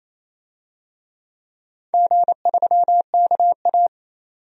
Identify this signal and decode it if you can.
Morse code, heard as a keyed tone, about 28 wpm, 700 Hz. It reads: G3KA
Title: DXCC-100 Call Signs - 4 Characters - 28wpm